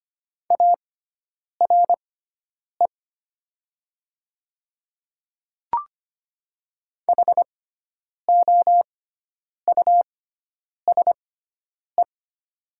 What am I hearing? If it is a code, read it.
Morse code, heard as a keyed tone, about 25 wpm, spaced as for 9 wpm, 700 Hz. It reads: ARE HOUSE
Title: 6x Character Spacing - Top 300 Words - 25wpm